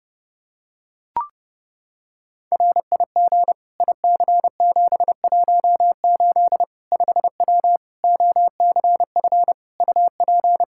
Morse code, heard as a keyed tone, about 30 wpm, 700 Hz. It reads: RIG IC718 5W OCF UP
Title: K1USN SST QSO Practice 30wpm